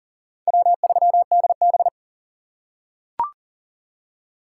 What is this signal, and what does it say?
Morse code, heard as a keyed tone, about 37 wpm, 700 Hz. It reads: W3DB
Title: USA Call Signs - 3x Character Spacing - 40wpm